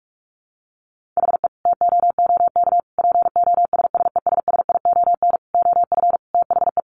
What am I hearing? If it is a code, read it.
Morse code, heard as a keyed tone, about 44 wpm, 700 Hz. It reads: HE TOOK POSSESSION OF THE
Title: Sentences from Top 1500 Words 45wpm